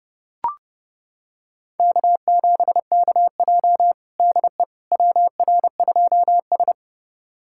Morse code, heard as a keyed tone, about 30 wpm, 700 Hz. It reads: K7KJ DE WR2S
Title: USA Callsign DE Callsign - 4 Characters - 30wpm